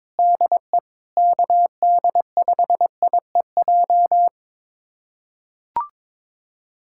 Morse code, heard as a keyed tone, about 22 wpm, 700 Hz. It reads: DE KD5IEJ
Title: USA Callsign DE Callsign - 6 Characters - 22wpm